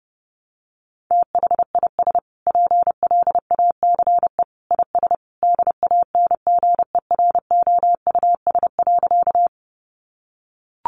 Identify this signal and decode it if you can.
Morse code, heard as a keyed tone, about 30 wpm, 700 Hz. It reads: THIS PLACE IS DANGEROUS.
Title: Sentences from Top 1200 Words 30wpm